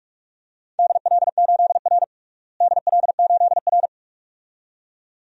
Morse code, heard as a keyed tone, about 46 wpm, 700 Hz. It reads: DL8R DL8R
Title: DXCC-100 Call Signs - 4 Characters - Repeated 2x - 45wpm